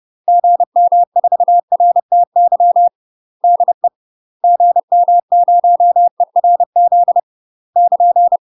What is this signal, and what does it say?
Morse code, heard as a keyed tone, about 30 wpm, 700 Hz. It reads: GM4RTY DE GM0ERZ <KN>
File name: UK MASTER QSOs TRAINING / 30WPM / UK MASTER QSOs 0004